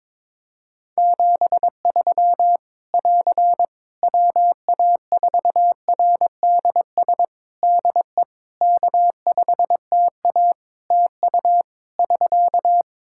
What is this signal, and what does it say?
Morse code, heard as a keyed tone, about 22 wpm, 700 Hz. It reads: 73 <AR> WA4RDS DE K5TA TU <SK>
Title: Practice QSO - Exchange 4 - Ending - 22wpm